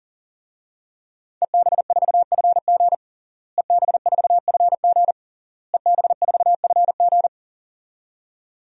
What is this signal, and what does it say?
Morse code, heard as a keyed tone, about 40 wpm, 700 Hz. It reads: EB4FG EB4FG EB4FG
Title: DXCC-100 Call Signs - 5 Characters - Repeated 3x - 40wpm